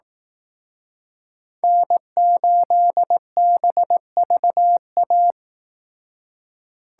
Morse code, heard as a keyed tone, about 18 wpm, 700 Hz. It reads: N8BVA
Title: Speed Racing - USA 1x3 Call Signs 15wpm